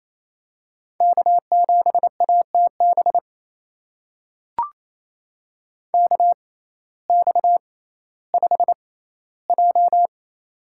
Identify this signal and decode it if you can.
Morse code, heard as a keyed tone, about 28 wpm, spaced as for 15 wpm, 700 Hz. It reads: K7ATB KX5J
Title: USA Call Signs - 6x Character Spacing - 28wpm